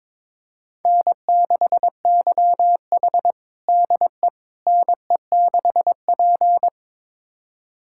Morse code, heard as a keyed tone, about 22 wpm, 700 Hz. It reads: N6YH DE NE6P
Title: USA Callsign DE Callsign - 4 Characters - 22wpm